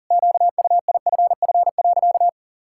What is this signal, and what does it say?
Morse code, heard as a keyed tone, about 37 wpm, 700 Hz. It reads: QUIFF.
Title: Pangram Challenge v2 - 40wpm